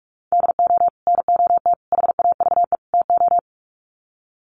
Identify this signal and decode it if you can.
Morse code, heard as a keyed tone, about 44 wpm, 700 Hz. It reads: DO NOT HAVE TO
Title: Sets of 4 Words 45wpm